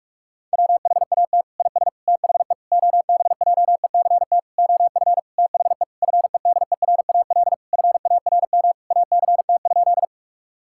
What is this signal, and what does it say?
Morse code, heard as a keyed tone, about 46 wpm, 700 Hz. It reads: WHAT IS THE OBJECT OF THE FEDERAL FARM ACT?
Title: Sentences from Top 1000 Words - Review Part 5 - 45wpm